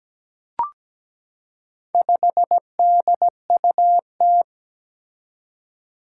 Morse code, heard as a keyed tone, about 17 wpm, 700 Hz. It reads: 5DUT
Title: Four Challenging Letter-Number Combinations 17wpm